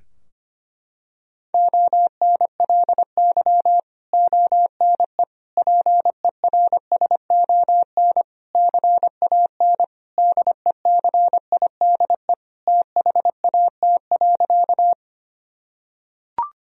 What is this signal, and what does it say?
Morse code, heard as a keyed tone, about 25 wpm, 700 Hz. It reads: ONLY ONE PERSON CAN DECIDE THAT.
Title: Sentences from Top 1000 Words - Review Part 3 - 25wpm